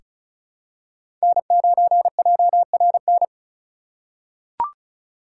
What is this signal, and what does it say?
Morse code, heard as a keyed tone, about 36 wpm, 700 Hz. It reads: N9JRN
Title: USA Call Signs - 5 Characters - 2x Character Spacing - 35wpm